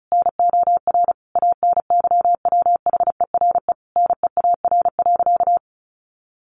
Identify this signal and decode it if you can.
Morse code, heard as a keyed tone, about 34 wpm, 700 Hz. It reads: NOR ANYWHERE NEAR.
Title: Sentences from Top 1500 Words 35wpm